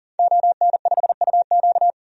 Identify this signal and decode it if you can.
Morse code, heard as a keyed tone, about 40 wpm, 700 Hz. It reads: ON5UQ
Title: DXCC-100 Call Signs - 5 Characters - 40wpm